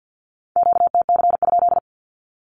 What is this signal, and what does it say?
Morse code, heard as a keyed tone, about 48 wpm, 700 Hz. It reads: QTC?
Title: Q-Codes 50wpm